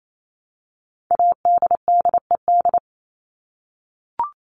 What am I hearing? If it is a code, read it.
Morse code, heard as a keyed tone, about 28 wpm, 700 Hz. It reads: ADDED
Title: ED Suffix - Top 100 Words with Base Words 28wpm